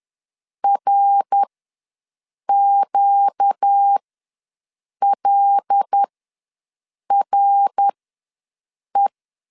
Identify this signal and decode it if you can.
Morse code, heard as a keyed tone, about 10 wpm, spaced as for 6 wpm, 800 Hz. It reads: RQLRE
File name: LEZIONI / 9 28 ws 10wpm 4124